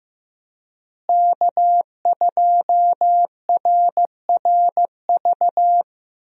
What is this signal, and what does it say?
Morse code, heard as a keyed tone, about 16 wpm, 700 Hz. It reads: K2RRV
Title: USA Call Signs - 5 Characters - Repeated 3x - 15wpm